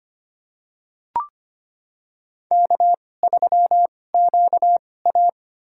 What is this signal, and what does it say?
Morse code, heard as a keyed tone, about 25 wpm, spaced as for 18 wpm, 700 Hz. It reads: K3QA
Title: USA Call Signs - 4 Characters - 2x Character Spacing - 25wpm